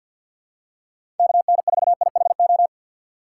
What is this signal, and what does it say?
Morse code, heard as a keyed tone, about 47 wpm, 700 Hz. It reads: KN4IHO
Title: USA Call Signs - 6x Character Spacing - 50wpm